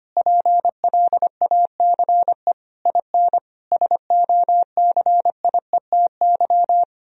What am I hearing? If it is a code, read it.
Morse code, heard as a keyed tone, about 25 wpm, 700 Hz. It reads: PLACE IN SOCIETY
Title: Sentences from Top 1000 Words - Review Part 4 - 25wpm